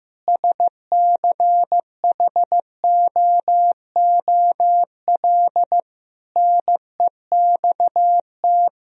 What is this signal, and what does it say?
Morse code, heard as a keyed tone, about 15 wpm, 700 Hz. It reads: SCHOOL NEXT
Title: Sentences from Top 1000 Words - Review Part 3 - 15wpm